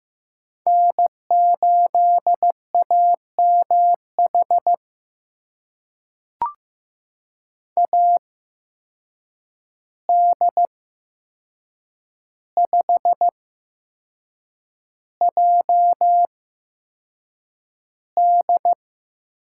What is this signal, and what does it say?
Morse code, heard as a keyed tone, about 15 wpm, spaced as for 6 wpm, 700 Hz. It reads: N8AMH AD5JD
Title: USA Call Signs - 5 Characters - 8x Character Spacing - 15wpm